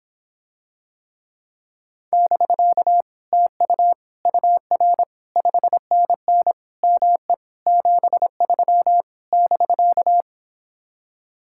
Morse code, heard as a keyed tone, about 26 wpm, 700 Hz. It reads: <BK> TU UR 5NN ME 73 <BK>